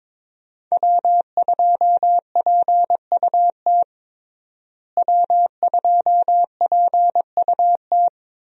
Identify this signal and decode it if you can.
Morse code, heard as a keyed tone, about 22 wpm, 700 Hz. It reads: W2PUT W2PUT
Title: USA Call Signs - 5 Characters - Repeated 2x - 22wpm